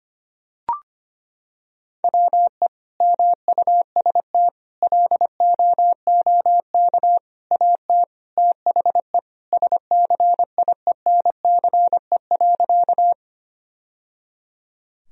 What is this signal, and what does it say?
Morse code, heard as a keyed tone, about 25 wpm, 700 Hz. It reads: WE MUST LOOK AT THE SCIENCE.
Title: Sentences from Top 1000 Words - Review Part 2 - 25wpm